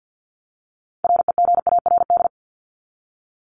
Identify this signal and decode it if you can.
Morse code, heard as a keyed tone, about 50 wpm, 700 Hz. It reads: REGARD
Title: ING Suffix - Top 100 Words with Base Words 50wpm